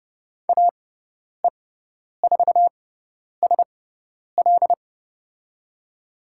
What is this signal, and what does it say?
Morse code, heard as a keyed tone, about 30 wpm, spaced as for 10 wpm, 700 Hz. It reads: AE4SL
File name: Lesson 18g - Letter m - Call Signs 30wpm